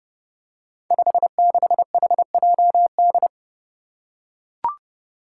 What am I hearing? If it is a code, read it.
Morse code, heard as a keyed tone, about 28 wpm, 700 Hz. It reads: 56HJD
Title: Five Challenging Letter-Number Combinations 30wpm